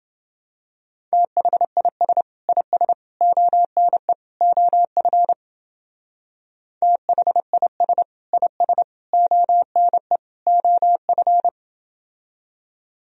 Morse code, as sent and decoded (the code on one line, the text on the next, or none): - .... .. ... / .. ... / --- -. / --- ..-. / - .... .. ... / .. ... / --- -. / --- ..-.
THIS IS ONE OF THIS IS ONE OF